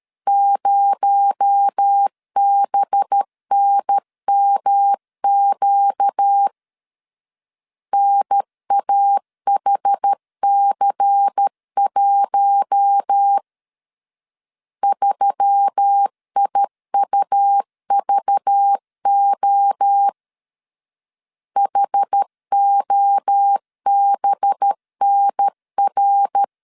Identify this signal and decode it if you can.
Morse code, heard as a keyed tone, about 13 wpm, 800 Hz. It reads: 0BNMQ NAHC1 3IUVO HOBNR
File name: LEZIONI / 6 PKX0 / 6 PKX0 12wpm 2231